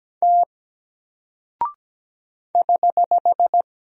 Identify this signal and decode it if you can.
Morse code, heard as a keyed tone, about 17 wpm, 700 Hz. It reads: T <HH>
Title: Single Character (Letter:Number:Punctuation:Prosign) Alt 17wpm